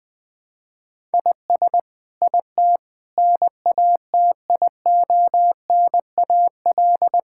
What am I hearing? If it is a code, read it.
Morse code, heard as a keyed tone, about 20 wpm, 700 Hz. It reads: IS IT NATIONAL
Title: Sentences from Top 1000 Words - Review Part 5 - 20wpm